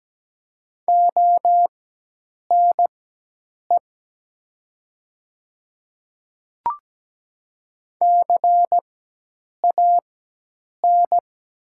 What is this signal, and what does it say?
Morse code, heard as a keyed tone, about 17 wpm, spaced as for 8 wpm, 700 Hz. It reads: ONE CAN
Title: 4x Character Spacing - Top 100 Words - 17wpm